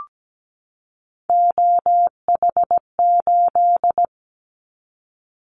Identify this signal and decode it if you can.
Morse code, heard as a keyed tone, about 17 wpm, 700 Hz. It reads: OH8
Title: Callsign Prefixes for Top 25 DXCC - 17wpm